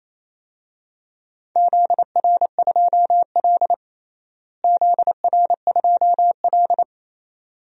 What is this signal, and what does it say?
Morse code, heard as a keyed tone, about 27 wpm, 700 Hz. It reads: ZR2L ZR2L
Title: DXCC-100 Call Signs - 4 Characters - Repeated 2x - 28wpm